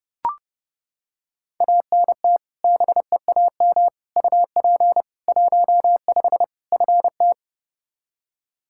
Morse code, heard as a keyed tone, about 30 wpm, 700 Hz. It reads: ANT BEAM UP 15 FT
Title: Practice QSO - Exchange 2 - Info - 30wpm